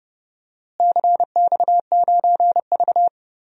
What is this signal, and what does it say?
Morse code, heard as a keyed tone, about 30 wpm, 700 Hz. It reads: CX9V
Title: DXCC-100 Call Signs - 4 Characters - Repeated 3x - 30wpm